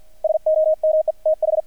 Morse code, heard as a keyed tone, about 48 wpm, 630 Hz. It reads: SOMETH
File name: archived / CRSnet